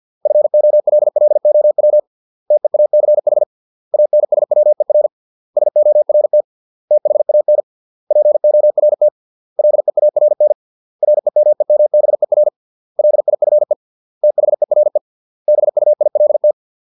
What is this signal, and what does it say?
Morse code, heard as a keyed tone, about 50 wpm, 600 Hz. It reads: FOLLOW TEACH ANSWER SORT THAN PORT LEARN REMEMBER LIFE THERE BUILT